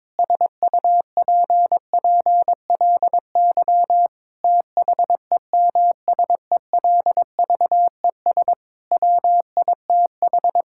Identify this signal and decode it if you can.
Morse code, heard as a keyed tone, about 22 wpm, 700 Hz. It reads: SUPPLY THEMSELVES WITH
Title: Sentences from Top 1000 Words - Review Part 5 - 22wpm